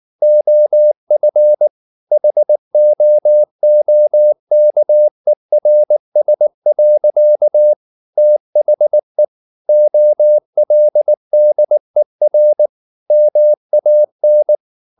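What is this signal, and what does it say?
Morse code, heard as a keyed tone, about 19 wpm, 600 Hz. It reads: OF HOOKERS. THE OLDER MAN
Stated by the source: Morse code at 19 wpm from CW med Gnister, reading Ken Follett's novel Never in English.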